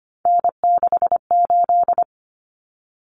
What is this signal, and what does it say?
Morse code, heard as a keyed tone, about 25 wpm, 700 Hz. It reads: N68
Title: Common Cut Numbers - 3-Digit 25wpm